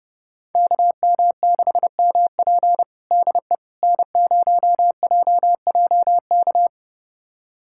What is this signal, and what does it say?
Morse code, heard as a keyed tone, about 32 wpm, 700 Hz. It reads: KM6MP DE N0JJK
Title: USA Callsign DE Callsign - 5 Characters - 30wpm